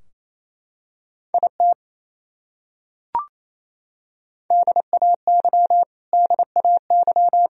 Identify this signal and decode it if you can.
Morse code, heard as a keyed tone, about 28 wpm, 700 Hz. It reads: IT DAY DAY
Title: Top 100 Words Repeated 28wpm